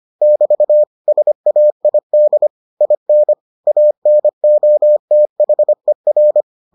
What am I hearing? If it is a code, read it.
Morse code, heard as a keyed tone, about 25 wpm, 600 Hz. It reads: <BT> SAID IN ANOTHER